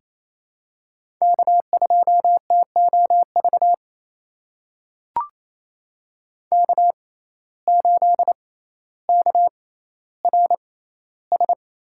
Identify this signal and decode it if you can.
Morse code, heard as a keyed tone, about 28 wpm, spaced as for 14 wpm, 700 Hz. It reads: K2TOV K8KRS
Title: USA Call Signs - 5 Characters - 6x Character Spacing - 28wpm